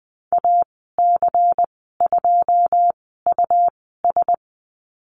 Morse code, heard as a keyed tone, about 20 wpm, spaced as for 15 wpm, 700 Hz. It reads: AC2US
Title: 122 Lesson 20g - Number 3 - Call Signs 20wpm